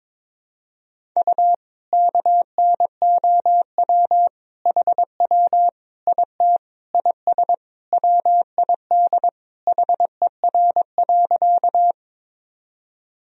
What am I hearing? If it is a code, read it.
Morse code, heard as a keyed tone, about 22 wpm, 700 Hz. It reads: U KNOW HW IT IS WID HER.
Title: Sentences from Top 100 Words and Abbreviations 22wpm